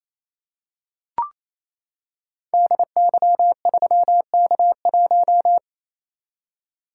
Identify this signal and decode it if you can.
Morse code, heard as a keyed tone, about 28 wpm, 700 Hz. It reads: DY3K1
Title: Code Groups - 5 Letter:Numbers - 28wpm